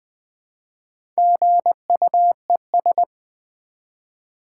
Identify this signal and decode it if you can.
Morse code, heard as a keyed tone, about 20 wpm, 700 Hz. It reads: GUES
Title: Build A Word - 1 Letter At A Time - 3 to 5-Letter Words - 20wpm